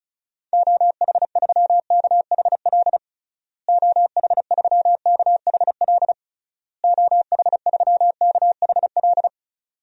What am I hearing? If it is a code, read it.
Morse code, heard as a keyed tone, about 36 wpm, 700 Hz. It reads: OH3KHL OH3KHL OH3KHL
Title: DXCC-100 Call Signs - 6 Characters - Repeated 3x - 35wpm